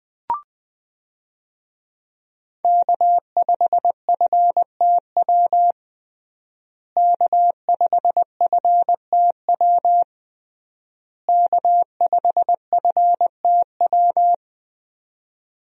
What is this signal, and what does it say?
Morse code, heard as a keyed tone, about 20 wpm, 700 Hz. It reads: K5FTW K5FTW K5FTW
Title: USA Call Signs - 5 Characters - Repeated 3x - 20wpm